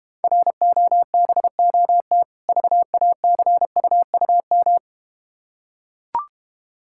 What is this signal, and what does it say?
Morse code, heard as a keyed tone, about 31 wpm, 700 Hz. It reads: ROBOT VACUUM